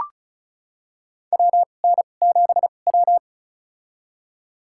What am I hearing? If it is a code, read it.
Morse code, heard as a keyed tone, about 36 wpm, spaced as for 25 wpm, 700 Hz. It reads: WN7W
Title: USA Call Signs - 4 Characters - 2x Character Spacing - 35wpm